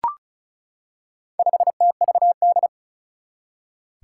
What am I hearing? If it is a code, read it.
Morse code, heard as a keyed tone, about 36 wpm, 700 Hz. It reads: 5TVD
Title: Four Challenging Letter-Number Combinations 35wpm